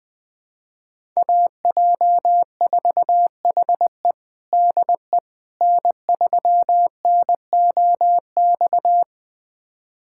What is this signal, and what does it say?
Morse code, heard as a keyed tone, about 20 wpm, 700 Hz. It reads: AJ4HE DE N3NOX
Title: USA Callsign DE Callsign - 5 Characters - 20wpm